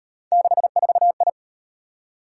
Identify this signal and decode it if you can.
Morse code, heard as a keyed tone, about 36 wpm, 700 Hz. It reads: 64I